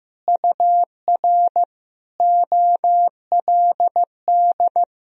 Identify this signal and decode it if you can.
Morse code, heard as a keyed tone, about 16 wpm, 700 Hz. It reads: UR OLD
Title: Sentences from Top 500 Words and Abbreviations 15wpm